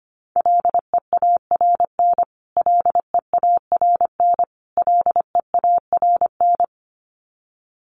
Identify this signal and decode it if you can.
Morse code, heard as a keyed tone, about 25 wpm, 700 Hz. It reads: LEARN LEARN LEARN
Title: Top 500 Words Repeated 25wpm